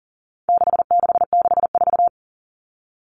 Morse code, heard as a keyed tone, about 40 wpm, 700 Hz. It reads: 6664
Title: Numbers - 4-Digit 40wpm